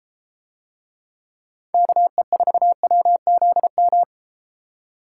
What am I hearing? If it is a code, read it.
Morse code, heard as a keyed tone, about 32 wpm, 700 Hz. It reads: KE4WZM